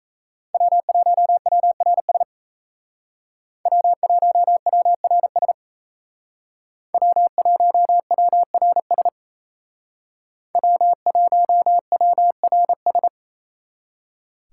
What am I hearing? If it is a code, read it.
Morse code, heard as a keyed tone, about 35 wpm, 700 Hz. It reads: W1WRS W1WRS W1WRS W1WRS
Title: Speed Racing - USA 1x3 Call Signs 28wpm